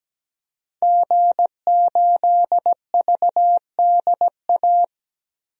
Morse code, heard as a keyed tone, about 17 wpm, 700 Hz. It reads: G8VDA
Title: DXCC-100 Call Signs - 5 Characters - 17wpm